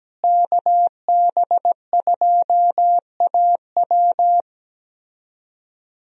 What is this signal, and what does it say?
Morse code, heard as a keyed tone, about 17 wpm, 700 Hz. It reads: KB2AW
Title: USA Call Signs - 5 Characters - Repeated 3x - 17wpm